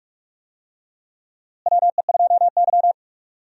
Morse code, heard as a keyed tone, about 48 wpm, 700 Hz. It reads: WE2Y
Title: USA Call Signs - 4 Characters - Repeated 2x - 45wpm